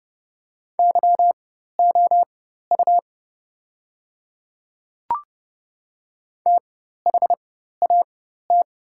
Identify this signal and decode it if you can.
Morse code, heard as a keyed tone, about 30 wpm, spaced as for 14 wpm, 700 Hz. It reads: YOU THAT
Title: 4x Character Spacing - Top 300 Words - 30wpm